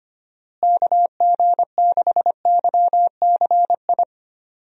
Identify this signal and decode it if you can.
Morse code, heard as a keyed tone, about 25 wpm, 700 Hz. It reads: KG6YCI
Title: USA Call Signs - 6 Characters - Repeated 3x - 25wpm